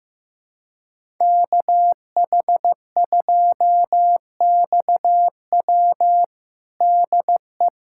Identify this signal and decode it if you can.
Morse code, heard as a keyed tone, about 15 wpm, 700 Hz. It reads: KH2XW DE
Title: USA Callsign DE Callsign 15wpm